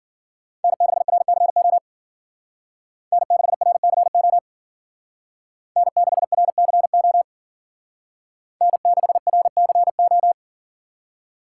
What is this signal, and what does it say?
Morse code, heard as a keyed tone, about 52 wpm, 700 Hz. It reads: N6RCO N6RCO N6RCO N6RCO